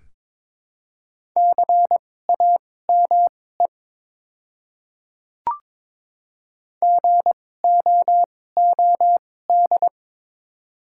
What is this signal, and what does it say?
Morse code, heard as a keyed tone, about 22 wpm, spaced as for 16 wpm, 700 Hz. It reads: CAME GOOD
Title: 2x Character Spacing - Top 200 Words - 22wpm